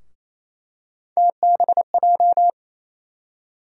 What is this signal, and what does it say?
Morse code, heard as a keyed tone, about 28 wpm, 700 Hz. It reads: TBJ
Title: USA Call Sign Postfixes - 3 Letters 28wpm